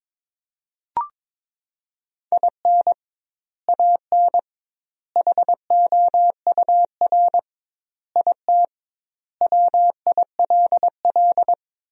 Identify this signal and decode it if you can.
Morse code, heard as a keyed tone, about 22 wpm, 700 Hz. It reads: IN AN HOUR IT WILL
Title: 2x Word-Spacing - Sentences from Top 400 Words - 22wpm